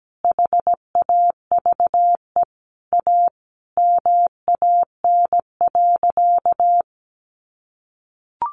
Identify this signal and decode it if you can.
Morse code, heard as a keyed tone, about 17 wpm, 700 Hz. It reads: HAVE A MAN.
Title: Sentences from Top 300 Words 17wpm